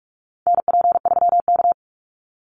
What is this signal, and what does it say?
Morse code, heard as a keyed tone, about 46 wpm, 700 Hz. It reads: NP3K
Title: Call Signs - 45wpm